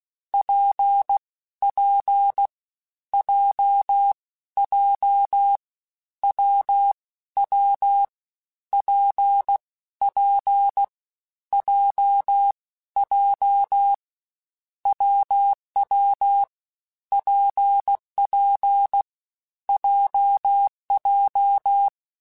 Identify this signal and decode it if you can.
Morse code, heard as a keyed tone, about 16 wpm, 800 Hz. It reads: P P J J W W P P J J WW PP JJ
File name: Difficult Combos / Difficult Combos-mp3s / WPJ WPJ1 @15